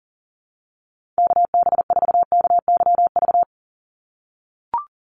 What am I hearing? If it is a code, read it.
Morse code, heard as a keyed tone, about 40 wpm, 700 Hz. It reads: KB4KYV
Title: USA Call Signs 40wpm